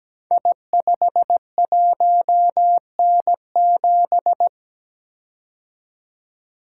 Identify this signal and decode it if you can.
Morse code, heard as a keyed tone, about 17 wpm, 700 Hz. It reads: I51N7
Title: Code Groups - 5 Letter:Numbers - 17wpm